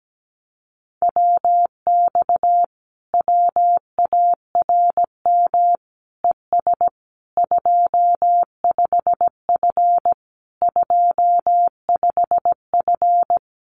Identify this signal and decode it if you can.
Morse code, heard as a keyed tone, about 17 wpm, 700 Hz. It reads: WX WARM ES 25F 25F
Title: Common QSO Phrases Encore 17wpm